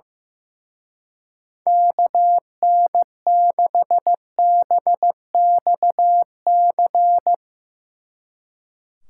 Morse code, heard as a keyed tone, about 16 wpm, 700 Hz. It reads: KN6BXC
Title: Speed Racing - USA 2x3 Call Signs 15wpm